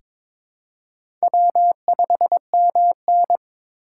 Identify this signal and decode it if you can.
Morse code, heard as a keyed tone, about 22 wpm, 700 Hz. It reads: W5MN
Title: USA Call Signs - 4 Characters - 4x Character Spacing - 22wpm